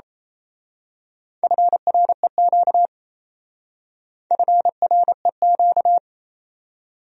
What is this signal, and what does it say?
Morse code, heard as a keyed tone, about 32 wpm, 700 Hz. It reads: FREQ FREQ
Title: Speed-Racing Abbreviations - 28wpm